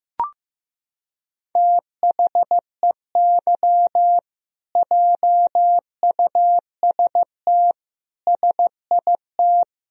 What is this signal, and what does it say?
Morse code, heard as a keyed tone, about 15 wpm, 700 Hz. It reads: THEY JUST SIT
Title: Sentences from Top 1000 Words - Review Part 3 - 15wpm